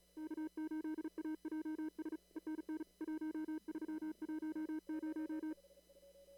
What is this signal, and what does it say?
Morse code, heard as a keyed tone, about 36 wpm, 310 Hz. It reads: K8AJS ENN 1310
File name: CW / kt5j wpx cw-op-k5pi